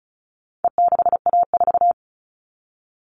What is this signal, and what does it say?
Morse code, heard as a keyed tone, about 34 wpm, 700 Hz. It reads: E6A4
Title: All Cut Numbers - 4-Digit 35wpm